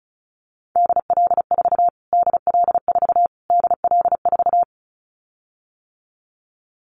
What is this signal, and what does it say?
Morse code, heard as a keyed tone, about 34 wpm, 700 Hz. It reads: DL4 DL4 DL4
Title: Callsign Prefixes for Top 25 DXCC - Repeated 3x - 35wpm